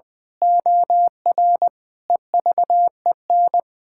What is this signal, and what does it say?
Morse code, heard as a keyed tone, about 20 wpm, 700 Hz. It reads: OR EVEN
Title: Pangram Challenge v2 - 20wpm